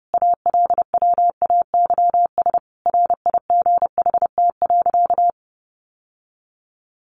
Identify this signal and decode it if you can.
Morse code, heard as a keyed tone, about 30 wpm, 700 Hz. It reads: ALWAYS RIGHT.
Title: Sentences from Top 1100 Words 30wpm